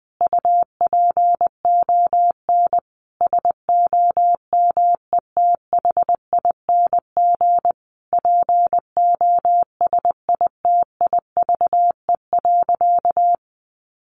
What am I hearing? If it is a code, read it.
Morse code, heard as a keyed tone, about 20 wpm, 700 Hz. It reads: UPON SOMETHING POSITIVE.
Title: Sentences from Top 1400 Words 20wpm